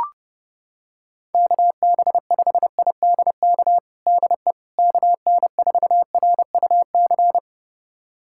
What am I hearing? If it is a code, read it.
Morse code, heard as a keyed tone, about 30 wpm, 700 Hz. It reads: KB5IDK DE KN4RUC
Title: USA Callsign DE Callsign - 6 Characters - 30wpm